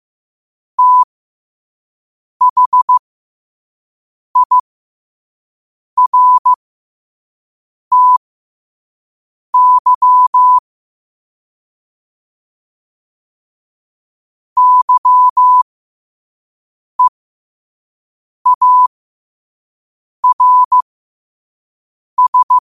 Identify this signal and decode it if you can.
Morse code, heard as a keyed tone, about 15 wpm, spaced as for 5 wpm, 1000 Hz. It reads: THIRTY YEARS